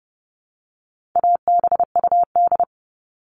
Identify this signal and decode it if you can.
Morse code, heard as a keyed tone, about 30 wpm, 700 Hz. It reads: ABUD
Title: All Cut Numbers - 4-Digit 30wpm